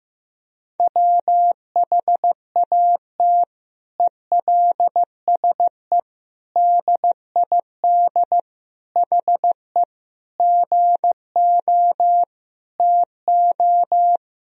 Morse code, heard as a keyed tone, about 15 wpm, 700 Hz. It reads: WHAT ELSE DID HE GO TO
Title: Sentences from Top 1000 Words - Review Part 1 - 15wpm